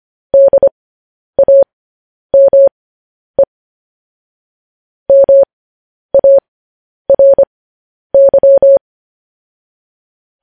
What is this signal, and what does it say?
Morse code, heard as a keyed tone, about 25 wpm, spaced as for 10 wpm, 550 Hz. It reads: DAME MARY
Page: news-headlines-in-morse-code-at-10-wpm